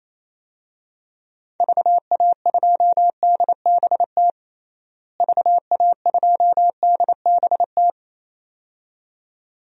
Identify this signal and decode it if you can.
Morse code, heard as a keyed tone, about 28 wpm, 700 Hz. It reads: VA2DBT VA2DBT
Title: DXCC-100 Call Signs - 6 Characters - Repeated 2x - 28wpm